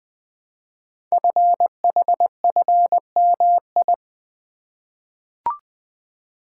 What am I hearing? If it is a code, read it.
Morse code, heard as a keyed tone, about 20 wpm, 700 Hz. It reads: FHFMI
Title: Code Groups - 5 Letter:Numbers - 20wpm